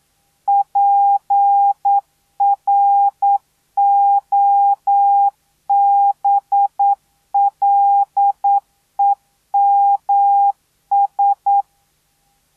Morse code, heard as a keyed tone, about 9 wpm, 800 Hz. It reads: PROBLEMS